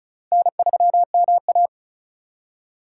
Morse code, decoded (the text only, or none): N3MA